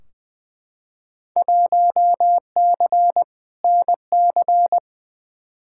Morse code, heard as a keyed tone, about 20 wpm, 700 Hz. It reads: 1C NC
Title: Field Day Exchanges - 20wpm